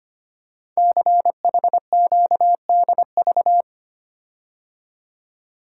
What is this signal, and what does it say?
Morse code, heard as a keyed tone, about 25 wpm, 700 Hz. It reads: CHQDV
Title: Code Groups - 5 Letter:Numbers - 25wpm